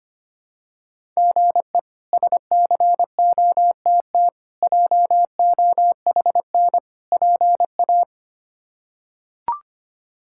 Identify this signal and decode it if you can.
Morse code, heard as a keyed tone, about 25 wpm, 700 Hz. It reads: GE SCOTT JOHN PA
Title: K1USN SST QSO Practice - 25wpm